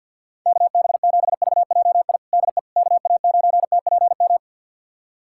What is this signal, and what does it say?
Morse code, heard as a keyed tone, about 51 wpm, 690 Hz. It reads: KB7VJI DE KA9TPM
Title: USA Callsign DE Callsign - 6 Characters - 50wpm